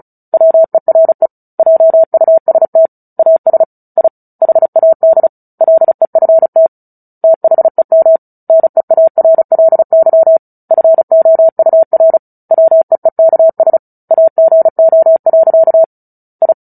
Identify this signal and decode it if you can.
Morse code, heard as a keyed tone, about 34 wpm, 650 Hz. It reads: WERE JUST AS I HAD LEFT THEM NEARLY FOUR WEEKS AGO. I
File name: Book Two / 09-Chapter 9 - 35 WPM